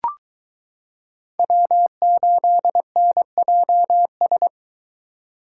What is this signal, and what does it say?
Morse code, heard as a keyed tone, about 23 wpm, 700 Hz. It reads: W8NJS